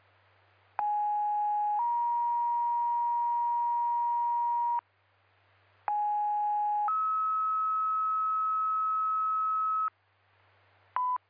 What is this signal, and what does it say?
Two-tone page, heard as a one-second tone, then a three-second tone, about 830 then 950 Hz, about 830 then 1290 Hz. It is heard from 0.8 to 9.9 s.
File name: Sparta EMS